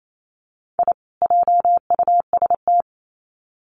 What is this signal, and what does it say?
Morse code, heard as a keyed tone, about 28 wpm, 700 Hz. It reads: I JUST
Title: Sets of 2 Words 28wpm